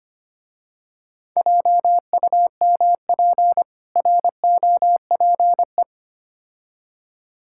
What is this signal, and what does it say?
Morse code, heard as a keyed tone, about 25 wpm, 700 Hz. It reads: JUMP ROPE
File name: practice / 25wpm / Difficult Words with Hints - 25wpm